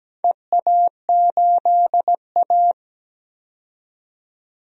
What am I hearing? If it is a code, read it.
Morse code, heard as a keyed tone, about 17 wpm, 700 Hz. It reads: EA8A
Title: DXCC-100 Call Signs - 4 Characters - 17wpm